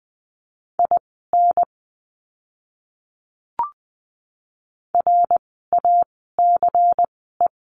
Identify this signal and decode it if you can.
Morse code, heard as a keyed tone, about 20 wpm, spaced as for 14 wpm, 700 Hz. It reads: IN RACE
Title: 094 Lesson 16f - Letter c - Words 20wpm